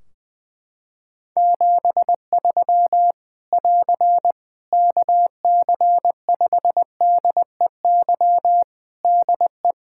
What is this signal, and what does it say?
Morse code, heard as a keyed tone, about 20 wpm, 700 Hz. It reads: 73 <AR> KC5DEY DE
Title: Practice QSO - Exchange 4 - Ending - 20wpm